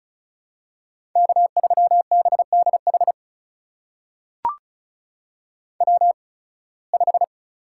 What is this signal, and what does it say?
Morse code, heard as a keyed tone, about 36 wpm, spaced as for 23 wpm, 700 Hz. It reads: K3BDH W5
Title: USA Call Signs - 5 Characters - 8x Character Spacing - 35wpm